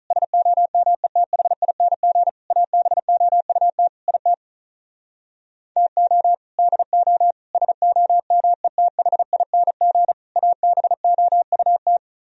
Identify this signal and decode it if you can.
Morse code, heard as a keyed tone, about 44 wpm, 700 Hz. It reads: SOMETHING ABOUT IT TO DO SOMETHING ABOUT